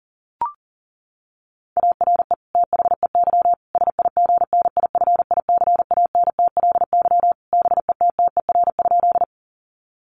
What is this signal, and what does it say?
Morse code, heard as a keyed tone, about 40 wpm, 700 Hz. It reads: ARE THEY SIGNIFICANTLY BETTER?
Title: Sentences from Top 1500 Words 40wpm